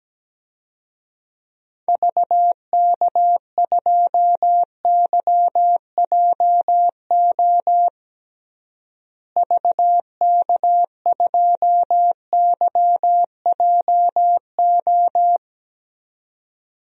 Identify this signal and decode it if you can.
Morse code, heard as a keyed tone, about 17 wpm, 700 Hz. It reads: VK2YJO VK2YJO
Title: DXCC-100 Call Signs - 6 Characters - Repeated 2x - 17wpm